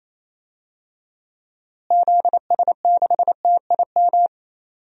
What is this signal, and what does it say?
Morse code, heard as a keyed tone, about 27 wpm, 700 Hz. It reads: ZS6TIM
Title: DXCC-100 Call Signs - 6 Characters - Repeated 2x - 28wpm